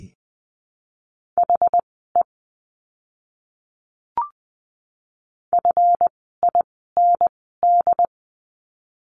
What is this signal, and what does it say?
Morse code, heard as a keyed tone, about 20 wpm, spaced as for 14 wpm, 700 Hz. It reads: HE FIND
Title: 160 Lesson 26f - Letter g - Words 20wpm